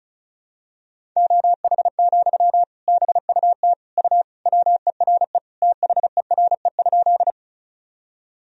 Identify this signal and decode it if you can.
Morse code, heard as a keyed tone, about 36 wpm, 700 Hz. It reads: OH, BUT U WERE THERE?
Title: Sentences from Top 500 Words and Abbreviations 35wpm